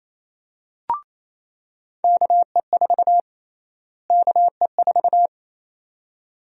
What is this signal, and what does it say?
Morse code, heard as a keyed tone, about 27 wpm, 700 Hz. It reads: KE4 KE4
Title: USA Call Sign Prefixes - Repeated 2x - 28wpm